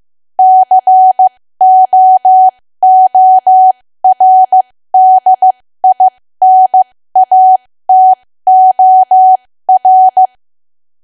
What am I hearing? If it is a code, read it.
Morse code, heard as a keyed tone, about 15 wpm, spaced as for 13 wpm, 750 Hz. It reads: COORDINATOR
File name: cwa / W1AW-Level2 / 13WPM